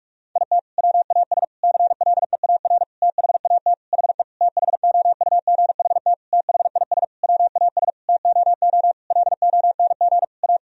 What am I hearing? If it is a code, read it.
Morse code, heard as a keyed tone, about 46 wpm, 700 Hz. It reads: IT WAS CLEAR THAT HE THOUGHT THIS WAS TOO LONG A
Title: Sentences from Top 1000 Words - Review Part 4 - 45wpm